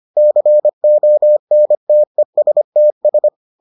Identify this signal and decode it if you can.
Morse code, heard as a keyed tone, about 25 wpm, 600 Hz. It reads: CONTESTS